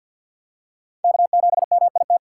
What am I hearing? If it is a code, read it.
Morse code, heard as a keyed tone, about 46 wpm, 700 Hz. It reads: K7MIT
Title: USA Call Signs - 6x Character Spacing - 50wpm